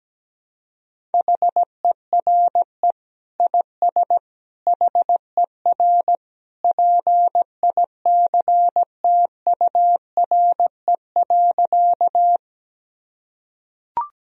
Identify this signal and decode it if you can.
Morse code, heard as a keyed tone, about 17 wpm, 700 Hz. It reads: HERE IS HER PICTURE.
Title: Sentences from Top 1000 Words - Review Part 1 - 17wpm